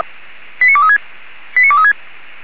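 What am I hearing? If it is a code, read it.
Five-tone selective call (ZVEI), heard as quick runs of five tones, tones of about 70 ms. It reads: 89236 89236
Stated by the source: Five-tone ZVEI selective call received by SDR.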